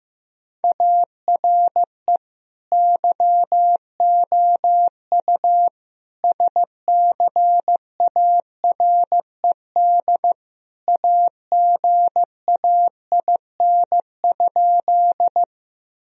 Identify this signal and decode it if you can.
Morse code, heard as a keyed tone, about 16 wpm, 700 Hz. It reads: ARE YOU SCARED AGAIN?
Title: Sentences from Top 1000 Words - Review Part 4 - 15wpm